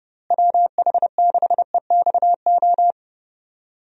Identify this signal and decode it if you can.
Morse code, heard as a keyed tone, about 32 wpm, 700 Hz. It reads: WH6EXO
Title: USA Call Signs - 6 Characters - Repeated 3x - 30wpm